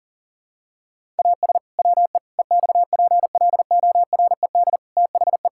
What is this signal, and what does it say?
Morse code, heard as a keyed tone, about 40 wpm, 700 Hz. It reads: AS WE EXPLORED THE
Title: Pangram Challenge v2 - 40wpm